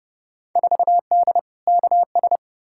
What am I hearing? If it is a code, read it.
Morse code, heard as a keyed tone, about 32 wpm, 700 Hz. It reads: 4D KS
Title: Field Day Exchanges Encore - 30wpm